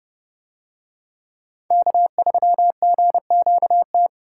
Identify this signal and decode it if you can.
Morse code, heard as a keyed tone, about 28 wpm, 700 Hz. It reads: K3GQT